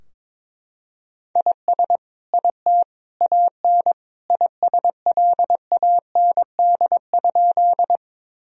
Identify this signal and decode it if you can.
Morse code, heard as a keyed tone, about 22 wpm, 700 Hz. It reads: IS IT AN ISLAND?
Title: Sentences from Top 1000 Words - Review Part 3 - 22wpm